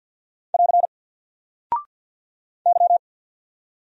Morse code, heard as a keyed tone, about 48 wpm, 700 Hz. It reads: <AR> Y
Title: Single Character (Letter:Number:Punctuation:Prosign) Alt 50wpm